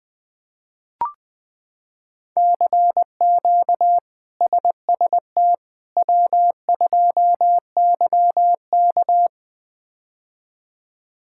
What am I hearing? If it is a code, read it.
Morse code, heard as a keyed tone, about 20 wpm, 700 Hz. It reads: CQ SST W2YK
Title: K1USN SST QSO Practice - 20wpm